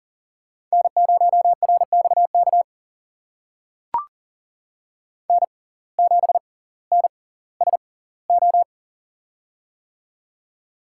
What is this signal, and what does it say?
Morse code, heard as a keyed tone, about 38 wpm, spaced as for 20 wpm, 700 Hz. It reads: N0RXK N7NSO
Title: USA Call Signs - 5 Characters - 6x Character Spacing - 40wpm